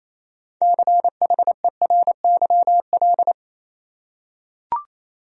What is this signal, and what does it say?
Morse code, heard as a keyed tone, about 28 wpm, 700 Hz. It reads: CHERYL
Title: USA Common Names - 28wpm